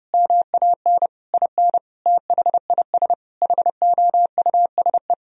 Morse code, heard as a keyed tone, about 32 wpm, 700 Hz. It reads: MAN IN THIS HOUSE
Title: Sentences from Top 1000 Words - Review Part 2 - 30wpm